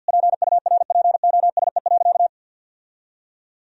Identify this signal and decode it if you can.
Morse code, heard as a keyed tone, about 50 wpm, 700 Hz. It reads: PURPOSE.
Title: Sentences from Top 1000 Words - Review Part 2 - 50wpm